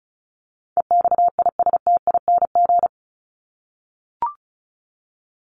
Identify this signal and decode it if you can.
Morse code, heard as a keyed tone, about 34 wpm, 700 Hz. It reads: EXISTING
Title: Top 1251-1500 Words 35wpm